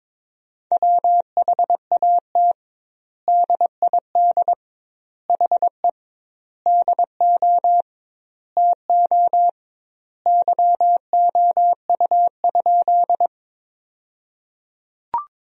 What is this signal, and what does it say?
Morse code, heard as a keyed tone, about 22 wpm, 700 Hz. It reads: WHAT DID HE DO TO YOU?
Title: 2x Word-Spacing - Sentences from Top 100 Words - 22wpm